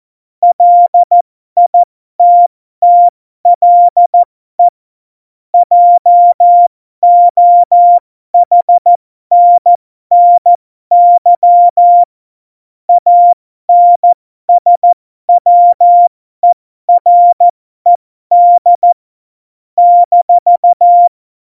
Morse code, heard as a keyed tone, about 14 wpm, spaced as for 12 wpm, 700 Hz. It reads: LITTLE JOHNNY ANSWERED -
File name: Morsekurs / Witze